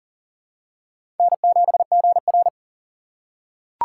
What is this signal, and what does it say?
Morse code, heard as a keyed tone, about 37 wpm, 700 Hz. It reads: N7GR
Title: USA Call Signs - 4 Characters - 6x Character Spacing - 40wpm